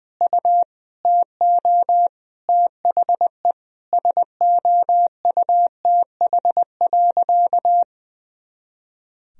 Morse code, heard as a keyed tone, about 20 wpm, 700 Hz. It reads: U TO THE SOUTH.
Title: Sentences from Top 400 Words and Abbreviations 20wpm